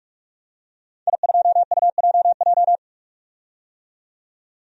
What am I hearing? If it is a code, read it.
Morse code, heard as a keyed tone, about 41 wpm, 700 Hz. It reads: I2UJJ
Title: DXCC-100 Call Signs - 5 Characters - 45wpm